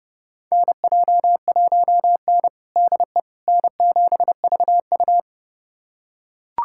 Morse code, heard as a keyed tone, about 30 wpm, 700 Hz. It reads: NJ1N DE N7VU
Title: USA Callsign DE Callsign - 4 Characters - 30wpm